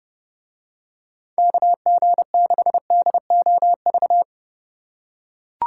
Morse code, heard as a keyed tone, about 31 wpm, 700 Hz. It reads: KG6DOV